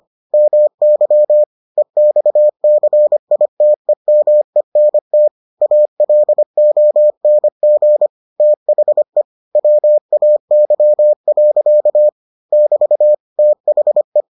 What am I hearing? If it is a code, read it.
Morse code, heard as a keyed tone, about 25 wpm, 600 Hz. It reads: MY EXCITEMENT ALONG THE WAY. <BT> THE